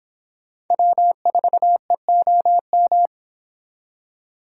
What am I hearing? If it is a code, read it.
Morse code, heard as a keyed tone, about 25 wpm, 700 Hz. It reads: W4EOM